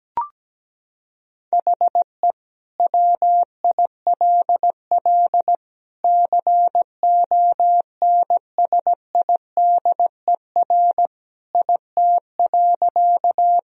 Morse code, heard as a keyed tone, about 17 wpm, 700 Hz. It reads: HE WILL CONSIDER IT.
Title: Sentences from Top 1000 Words - Review Part 4 - 17wpm